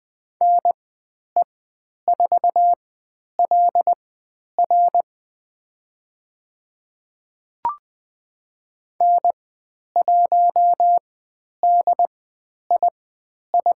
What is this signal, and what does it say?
Morse code, heard as a keyed tone, about 20 wpm, spaced as for 10 wpm, 700 Hz. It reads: NE4LR N1DII
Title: 095 Lesson 16g - Letter c - Call Signs 20wpm